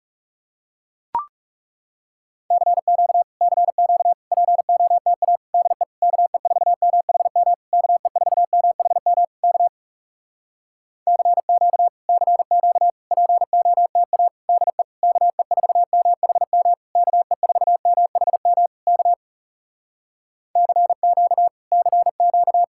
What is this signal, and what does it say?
Morse code, heard as a keyed tone, about 37 wpm, 700 Hz. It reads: CQ CQ POTA DE KE4MHM KE4MHM K CQ CQ POTA DE KE4MHM KE4MHM K CQ CQ
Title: Speed-Racing Common QSO Phrases Encore - 30wpm